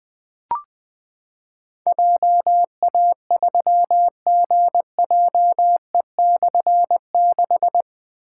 Morse code, heard as a keyed tone, about 21 wpm, 700 Hz. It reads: JA3GJE/6
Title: Difficult Call Signs 20wpm